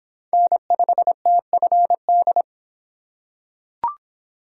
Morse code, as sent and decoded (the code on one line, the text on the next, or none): -. ..... - ..-. -..
N5TFD